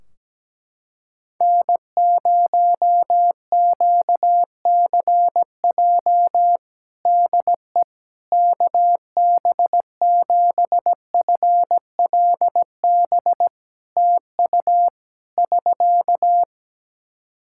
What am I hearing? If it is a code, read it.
Morse code, heard as a keyed tone, about 17 wpm, 700 Hz. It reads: N0QCJ DE KB7FLB TU <SK>
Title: Practice QSO - Exchange 4 - Ending - 17wpm